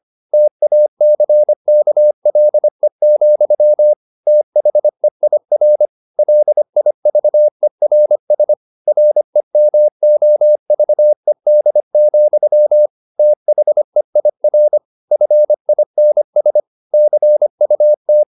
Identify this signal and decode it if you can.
Morse code, heard as a keyed tone, about 25 wpm, 600 Hz. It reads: TACKLE, THEIR LIVERS REMOVED, THEIR FINS CUT
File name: the old man and the sea part 1 25wpm